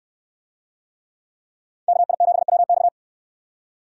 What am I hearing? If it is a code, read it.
Morse code, heard as a keyed tone, about 62 wpm, 700 Hz. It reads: KI7WX